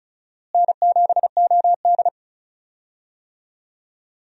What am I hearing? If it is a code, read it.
Morse code, heard as a keyed tone, about 36 wpm, 700 Hz. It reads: N7OD
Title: USA Call Signs - 4 Characters - Repeated 3x - 35wpm